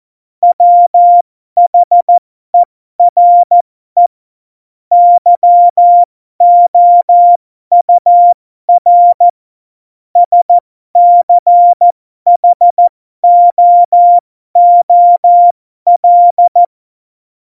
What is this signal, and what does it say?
Morse code, heard as a keyed tone, about 14 wpm, spaced as for 12 wpm, 700 Hz. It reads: WHERE YOUR SCHOOL
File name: Morsekurs / Witze